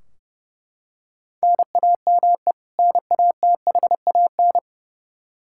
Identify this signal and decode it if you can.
Morse code, heard as a keyed tone, about 32 wpm, 700 Hz. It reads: NAME NATHAN
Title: Practice QSO - Exchange 1 - Beginning - 30wpm